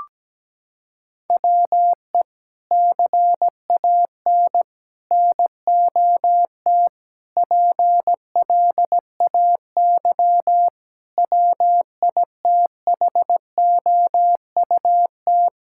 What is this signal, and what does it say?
Morse code, heard as a keyed tone, about 17 wpm, 700 Hz. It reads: WE CAN NOT PLAY WITHOUT
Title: Sentences from Top 1000 Words - Review Part 6 - 17wpm